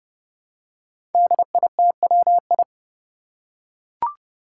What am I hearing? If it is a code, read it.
Morse code, heard as a keyed tone, about 30 wpm, 700 Hz. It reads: DITWI
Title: Five Challenging Letter-Number Combinations 30wpm